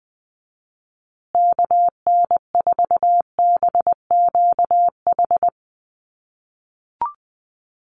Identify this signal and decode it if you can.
Morse code, heard as a keyed tone, about 20 wpm, 700 Hz. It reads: KN4BQH
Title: USA Call Signs Repeated 2x 20wpm